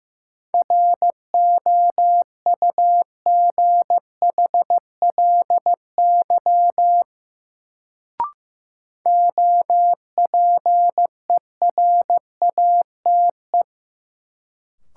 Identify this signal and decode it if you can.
Morse code, heard as a keyed tone, about 15 wpm, 700 Hz. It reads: ROUGHLY OPERATE
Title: Sentences from Top 1500 Words - New Word Drill Alt - 15wpm